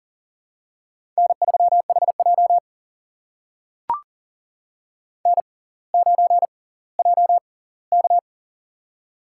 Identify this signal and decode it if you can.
Morse code, heard as a keyed tone, about 39 wpm, spaced as for 20 wpm, 700 Hz. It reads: N3HJ N9JK
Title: USA Call Signs - 4 Characters - 6x Character Spacing - 40wpm